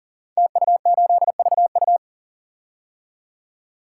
Morse code, heard as a keyed tone, about 42 wpm, 700 Hz. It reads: TU8VU
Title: Code Groups - 5 Letter:Numbers - 40wpm